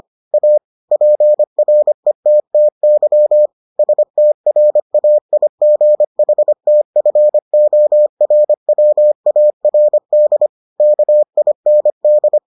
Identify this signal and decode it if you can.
Morse code, heard as a keyed tone, about 25 wpm, 600 Hz. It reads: A PRETTY STRAIGHTFORWARD KIND